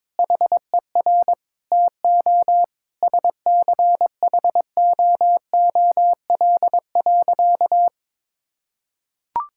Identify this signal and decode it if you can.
Morse code, heard as a keyed tone, about 21 wpm, 700 Hz. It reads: HER TO SCHOOL.
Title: Sentences from Top 1000 Words - Review Part 5 - 22wpm